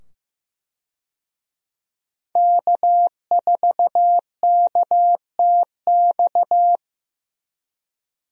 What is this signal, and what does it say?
Morse code, heard as a keyed tone, about 15 wpm, 700 Hz. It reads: K4KTX
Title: USA Call Signs - 5 Characters - 15wpm